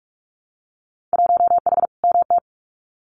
Morse code, heard as a keyed tone, about 46 wpm, 700 Hz. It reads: 1H MT
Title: Winter Field Day Exchanges Encore 45wpm